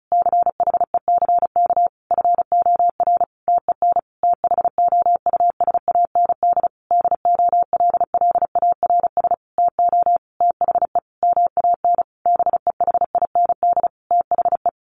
Morse code, heard as a keyed tone, about 34 wpm, 700 Hz. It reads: CHECK FOR TEN THOUSAND DOLLARS TO THE MAN BEHIND THE
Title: A Sound Of Thunder 35wpm